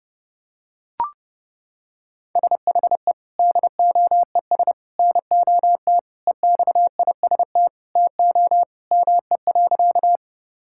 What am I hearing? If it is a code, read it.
Morse code, heard as a keyed tone, about 30 wpm, 700 Hz. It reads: SHE DOES NOT EXIST TO ME.
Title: Sentences from Top 1000 Words - Review Part 5 - 30wpm